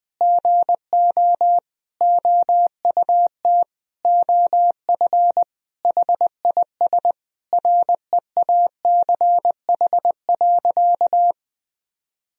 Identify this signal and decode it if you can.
Morse code, heard as a keyed tone, about 21 wpm, 700 Hz. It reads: GO OUT OF HIS REACH.
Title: Sentences from Top 1000 Words - Review Part 2 - 20wpm